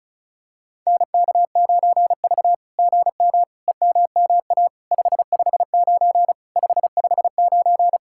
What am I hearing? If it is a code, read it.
Morse code, heard as a keyed tone, about 33 wpm, 700 Hz. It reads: NK9V GM EMMA 559 559
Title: SOTA QSO Practice 35wpm